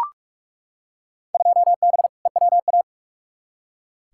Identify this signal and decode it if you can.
Morse code, heard as a keyed tone, about 41 wpm, 700 Hz. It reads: 2B EWA
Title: Field Day Exchanges - 45wpm